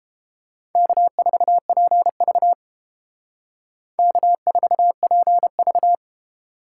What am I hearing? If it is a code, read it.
Morse code, heard as a keyed tone, about 30 wpm, 700 Hz. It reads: K4PV K4PV